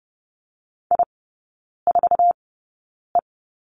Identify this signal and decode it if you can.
Morse code, heard as a keyed tone, about 30 wpm, spaced as for 9 wpm, 700 Hz. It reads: I4E
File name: 081 Lesson 14e - Number 2 - Three Characters 30wpm